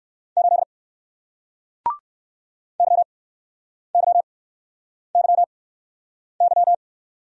Morse code, heard as a keyed tone, about 58 wpm, spaced as for 9 wpm, 700 Hz. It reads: 8 YYYY